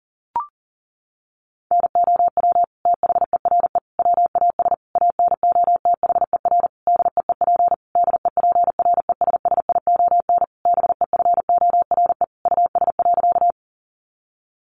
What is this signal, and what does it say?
Morse code, heard as a keyed tone, about 40 wpm, 700 Hz. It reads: NOW THERE WAS ANOTHER DEEP DEPRESSION BEFORE US.
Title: Sentences from Top 1300 Words 40wpm